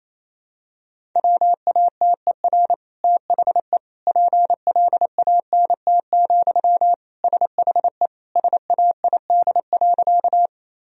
Morse code, heard as a keyed tone, about 27 wpm, 700 Hz. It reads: WATER THE PLANT, SHE SAID.
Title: Sentences from Top 1000 Words - Review Part 1 - 28wpm